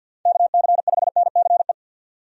Morse code, heard as a keyed tone, about 53 wpm, 700 Hz. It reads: KC5NCE
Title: USA Call Signs - 6 Characters - 50wpm